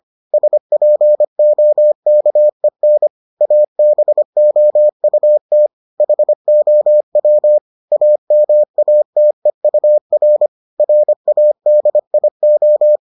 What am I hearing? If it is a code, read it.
Morse code, heard as a keyed tone, about 25 wpm, 600 Hz. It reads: SPOKEN ABOUT HOW AMATEUR RADIO